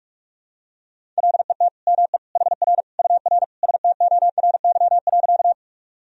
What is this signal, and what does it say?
Morse code, heard as a keyed tone, about 43 wpm, 700 Hz. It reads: LET ME HR UR STORY.
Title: Sentences from Top 300 Words and Abbreviations 45wpm